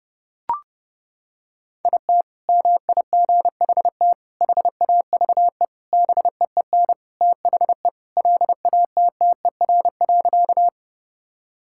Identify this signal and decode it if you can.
Morse code, heard as a keyed tone, about 32 wpm, 700 Hz. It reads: IT MIGHT HAVE BEEN THE LATTER.
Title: Sentences from Top 1600 Words 30wpm